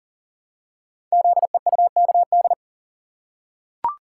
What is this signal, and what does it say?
Morse code, heard as a keyed tone, about 40 wpm, 690 Hz. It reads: ZEUKD
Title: Code Groups - 5 Letters - 40wpm